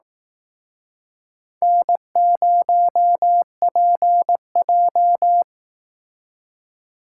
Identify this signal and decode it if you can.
Morse code, heard as a keyed tone, about 18 wpm, 700 Hz. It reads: N0PJ
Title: Speed Racing - USA 1x2 Call Signs 15wpm